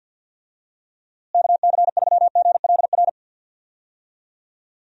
Morse code, heard as a keyed tone, about 51 wpm, 700 Hz. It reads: KC3GLR
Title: USA Call Signs - 6 Characters - Repeated 3x - 50wpm